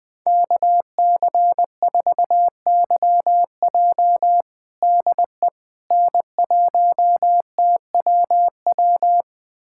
Morse code, heard as a keyed tone, about 20 wpm, 700 Hz. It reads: KC4YJ DE N1TWW
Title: USA Callsign DE Callsign - 5 Characters - 20wpm